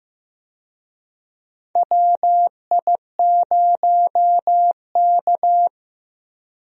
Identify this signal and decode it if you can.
Morse code, heard as a keyed tone, about 15 wpm, 700 Hz. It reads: WI0K
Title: USA Call Signs - 4 Characters - Repeated 2x - 15wpm